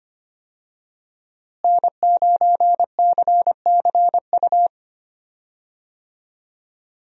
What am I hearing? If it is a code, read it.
Morse code, heard as a keyed tone, about 25 wpm, 700 Hz. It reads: N9CCU
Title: USA Call Signs - 5 Characters - Repeated 3x - 25wpm